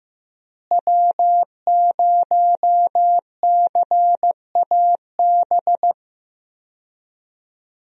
Morse code, heard as a keyed tone, about 15 wpm, 700 Hz. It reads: W0CAB
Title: USA Call Signs - 5 Characters - Repeated 3x - 15wpm